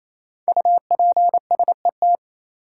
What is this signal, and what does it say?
Morse code, heard as a keyed tone, about 28 wpm, 700 Hz. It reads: UPSET
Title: Sentences from Top 800 Words - New Word Drill - 28wpm